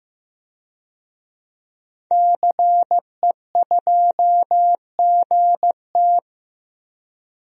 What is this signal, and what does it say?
Morse code, heard as a keyed tone, about 16 wpm, 700 Hz. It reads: CE2GT
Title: DXCC-100 Call Signs - 5 Characters - 15wpm